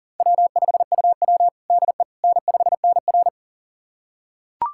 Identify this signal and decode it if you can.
Morse code, heard as a keyed tone, about 40 wpm, 700 Hz. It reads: W5UW DE N5NR
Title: USA Callsign DE Callsign - 4 Characters - 40wpm